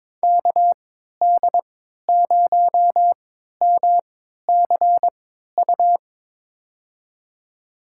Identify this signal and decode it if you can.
Morse code, heard as a keyed tone, about 22 wpm, spaced as for 13 wpm, 700 Hz. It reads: KD0MCU
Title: USA Call Signs - 6 Characters - 3x Character Spacing - 22wpm